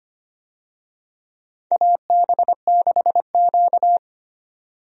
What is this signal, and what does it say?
Morse code, heard as a keyed tone, about 25 wpm, 700 Hz. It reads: AB6Q
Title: USA Call Signs - 4 Characters - 25wpm